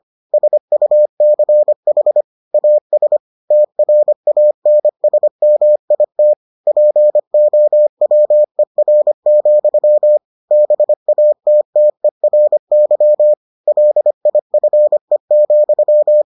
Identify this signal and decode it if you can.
Morse code, heard as a keyed tone, about 25 wpm, 600 Hz. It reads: SUCH AS TRANSMIT POWER, BATTERY LIFE,